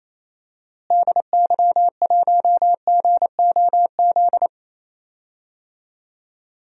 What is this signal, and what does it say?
Morse code, heard as a keyed tone, about 28 wpm, 700 Hz. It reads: DY1GOZ
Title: DXCC-100 Call Signs - 6 Characters - Repeated 2x - 28wpm